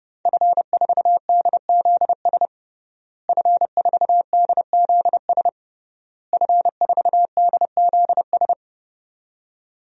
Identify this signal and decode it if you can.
Morse code, heard as a keyed tone, about 32 wpm, 700 Hz. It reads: F4DZS F4DZS F4DZS
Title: DXCC-100 Call Signs - 5 Characters - Repeated 3x - 30wpm